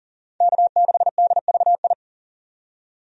Morse code, heard as a keyed tone, about 40 wpm, 700 Hz. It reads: K6DVI
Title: USA Call Signs - 5 Characters - 40wpm